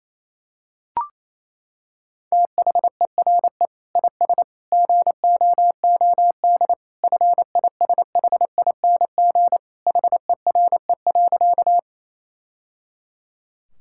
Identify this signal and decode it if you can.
Morse code, heard as a keyed tone, about 28 wpm, 700 Hz. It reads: THERE IS GOOD FISHING HERE.
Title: Sentences from Top 1000 Words - Review Part 5 - 28wpm